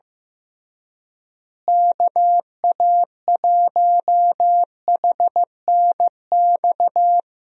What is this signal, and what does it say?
Morse code, heard as a keyed tone, about 15 wpm, 700 Hz. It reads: KA1HNX
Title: USA Call Signs - 6 Characters - Repeated 2x - 15wpm